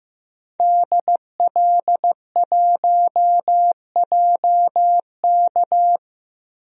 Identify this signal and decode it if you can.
Morse code, heard as a keyed tone, about 15 wpm, 700 Hz. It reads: DL1JK
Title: DXCC-100 Call Signs - 5 Characters - 15wpm